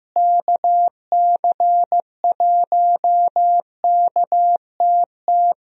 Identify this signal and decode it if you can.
Morse code, heard as a keyed tone, about 15 wpm, 700 Hz. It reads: KC1KTT
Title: USA Call Signs - 6 Characters - Repeated 3x - 15wpm